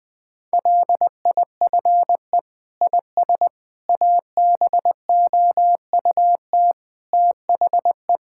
Morse code, heard as a keyed tone, about 20 wpm, 700 Hz. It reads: LIFE IS ABOUT THE
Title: Sentences with 6-Word Phrases - 20wpm